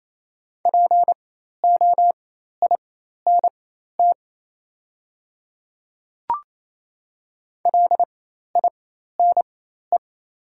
Morse code, heard as a keyed tone, about 28 wpm, spaced as for 13 wpm, 700 Hz. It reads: POINT LINE
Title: 4x Character Spacing - Top 200 Words - 28wpm